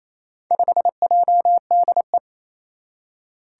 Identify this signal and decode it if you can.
Morse code, heard as a keyed tone, about 27 wpm, 700 Hz. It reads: 5JDE
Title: Four Challenging Letter-Number Combinations 28wpm